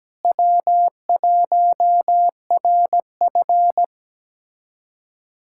Morse code, heard as a keyed tone, about 17 wpm, 700 Hz. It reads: W1RF
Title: USA Callsign DE Callsign - 4 Characters - 17wpm